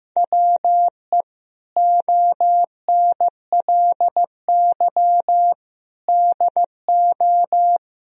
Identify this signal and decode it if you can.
Morse code, heard as a keyed tone, about 16 wpm, 700 Hz. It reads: WE ONLY DO
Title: Sentences from Top 1000 Words - Review Part 4 - 15wpm